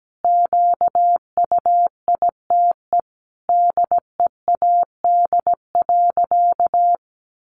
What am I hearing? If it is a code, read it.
Morse code, heard as a keyed tone, about 17 wpm, 700 Hz. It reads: QUITE DEAD.
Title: Sentences from Top 1500 Words 17wpm